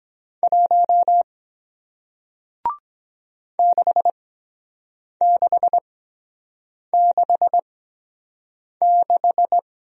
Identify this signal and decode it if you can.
Morse code, heard as a keyed tone, about 23 wpm, spaced as for 7 wpm, 700 Hz. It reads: 1 6666